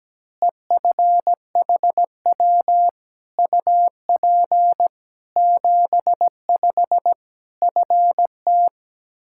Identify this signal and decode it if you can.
Morse code, heard as a keyed tone, about 17 wpm, 700 Hz. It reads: EFHW UP 75 FT
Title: Practice QSO - Exchange 2 - Info - 17wpm